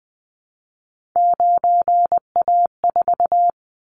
Morse code, heard as a keyed tone, about 20 wpm, 700 Hz. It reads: 9A4
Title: Callsign Prefixes for Top 50 DXCC - 20wpm